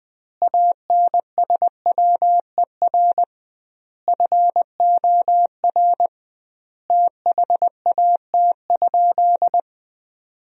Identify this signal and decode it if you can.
Morse code, heard as a keyed tone, about 20 wpm, 700 Hz. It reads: ANSWER FOR THAT?
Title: 2x Word-Spacing - Sentences from Top 300 Words - 20wpm